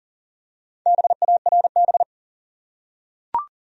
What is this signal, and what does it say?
Morse code, heard as a keyed tone, about 37 wpm, 700 Hz. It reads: BARB
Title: USA Common Names - Max 4 Letters - 40wpm